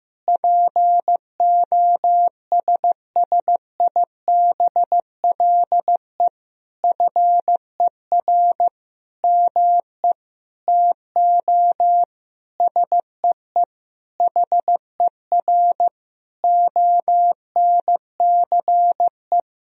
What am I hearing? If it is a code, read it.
Morse code, heard as a keyed tone, about 15 wpm, 700 Hz. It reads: POSSIBLE FER ME TO SEE HER ONCE
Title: Sentences from Top 500 Words and Abbreviations 15wpm